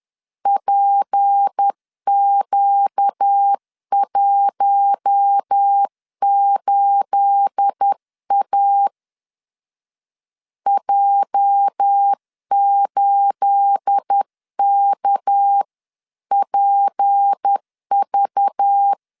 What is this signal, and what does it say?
Morse code, heard as a keyed tone, about 10 wpm, 800 Hz. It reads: PQ18A J8KPV
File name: LEZIONI / 9 28 10wpm 2842